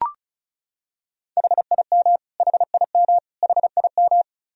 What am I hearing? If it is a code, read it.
Morse code, heard as a keyed tone, about 36 wpm, 700 Hz. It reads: HIM HIM HIM
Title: Top 100 Words Repeated 35wpm